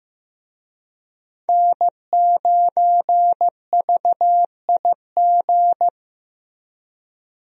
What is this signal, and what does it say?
Morse code, heard as a keyed tone, about 15 wpm, 700 Hz. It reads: N9VIG
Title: USA Call Signs - 5 Characters - 15wpm